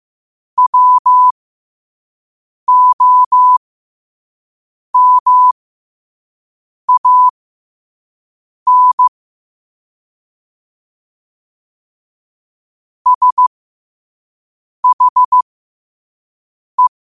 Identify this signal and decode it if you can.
Morse code, heard as a keyed tone, about 15 wpm, spaced as for 5 wpm, 1000 Hz. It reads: WOMAN SHE
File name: qotdfiles / 5wpm